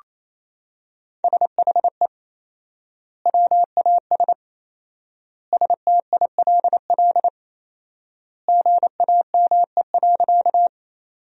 Morse code, heard as a keyed tone, about 28 wpm, 700 Hz. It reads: SHE WAS STILL GAME.
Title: 4x Word-Spacing - Sentences from Top 500 Words - 28wpm